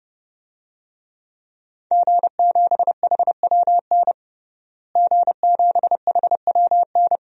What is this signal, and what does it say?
Morse code, heard as a keyed tone, about 32 wpm, 700 Hz. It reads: G7HWN G7HWN
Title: DXCC-100 Call Signs - 5 Characters - Repeated 2x - 30wpm